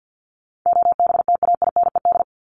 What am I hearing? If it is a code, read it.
Morse code, heard as a keyed tone, about 48 wpm, 700 Hz. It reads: OBTAINED
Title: ED Suffix - Top 100 Words with Base Words 50wpm